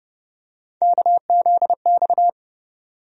Morse code, heard as a keyed tone, about 31 wpm, 700 Hz. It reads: KZX
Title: USA Call Sign Postfixes - 3 Letters 30wpm